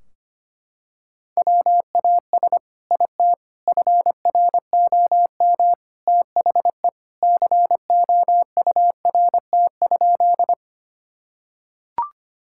Morse code, heard as a keyed tone, about 25 wpm, 700 Hz. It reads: WAS IT FROM THE COURT?
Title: Sentences from Top 1600 Words 25wpm